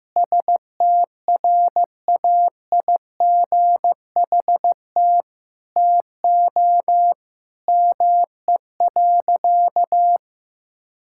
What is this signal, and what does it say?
Morse code, heard as a keyed tone, about 15 wpm, 700 Hz. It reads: STRAIGHT TO ME.
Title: Sentences from Top 1000 Words - Review Part 6 - 15wpm